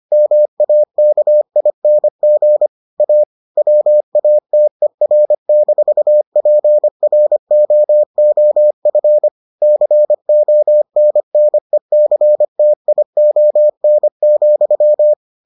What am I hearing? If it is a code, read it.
Morse code, heard as a keyed tone, about 25 wpm, 600 Hz. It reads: MAKING A WATER-PROOF CONNECTION,